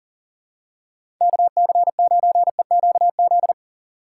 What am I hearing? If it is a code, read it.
Morse code, heard as a keyed tone, about 40 wpm, 690 Hz. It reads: KC9EQZ
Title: USA Call Signs - 3x Character Spacing - 40wpm